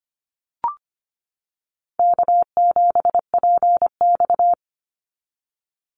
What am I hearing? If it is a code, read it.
Morse code, heard as a keyed tone, about 25 wpm, 700 Hz. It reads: K7PX
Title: NA SOTA Activator Callsigns 25wpm